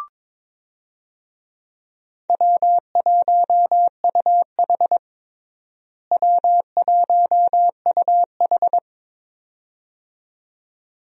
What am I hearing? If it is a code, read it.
Morse code, heard as a keyed tone, about 22 wpm, 700 Hz. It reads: W1UH W1UH
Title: USA Call Signs - 4 Characters - Repeated 2x - 22wpm